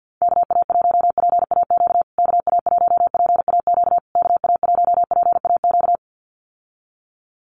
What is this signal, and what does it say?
Morse code, heard as a keyed tone, about 51 wpm, 700 Hz. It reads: KA1PAQ KA1PAQ KA1PAQ
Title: USA Call Signs Repeated 3x 50wpm